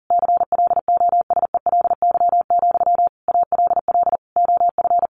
Morse code, heard as a keyed tone, about 40 wpm, 700 Hz. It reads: CLOSELY, ALL OF
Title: Sentences from Top 1400 Words 40wpm